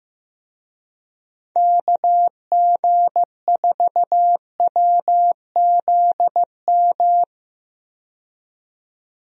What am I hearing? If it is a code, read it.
Morse code, heard as a keyed tone, about 15 wpm, 700 Hz. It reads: KG4WZM
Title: USA Call Signs - 6 Characters - 15wpm